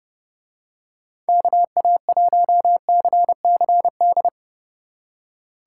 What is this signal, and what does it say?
Morse code, heard as a keyed tone, about 30 wpm, 700 Hz. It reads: KA1CCD